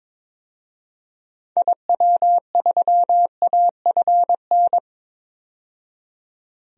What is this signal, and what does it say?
Morse code, heard as a keyed tone, about 22 wpm, 700 Hz. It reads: IW3AFN
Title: DXCC-100 Call Signs - 6 Characters - 22wpm